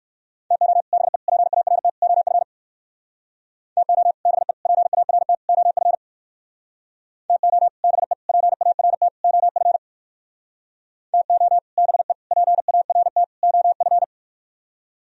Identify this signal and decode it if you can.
Morse code, heard as a keyed tone, about 58 wpm, 700 Hz. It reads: TO BE PART OF TO BE PART OF TO BE PART OF TO BE PART OF